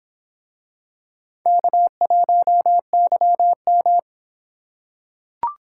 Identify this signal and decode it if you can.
Morse code, heard as a keyed tone, about 26 wpm, 700 Hz. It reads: K1YM